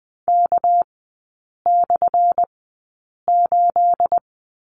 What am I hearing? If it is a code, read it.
Morse code, heard as a keyed tone, about 20 wpm, spaced as for 8 wpm, 700 Hz. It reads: K/8
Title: 218 Lesson 35e - Number 0 - Three Characters 20wpm